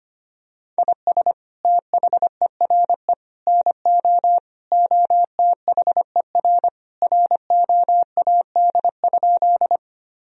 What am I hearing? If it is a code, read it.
Morse code, heard as a keyed tone, about 25 wpm, 700 Hz. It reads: IS THERE NO OTHER ROAD?
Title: Sentences from Top 1000 Words - Review Part 2 - 25wpm